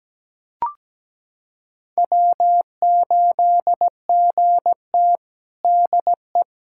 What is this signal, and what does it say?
Morse code, heard as a keyed tone, about 17 wpm, 700 Hz. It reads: W8GT DE
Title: USA Callsign DE Callsign - 4 Characters - 17wpm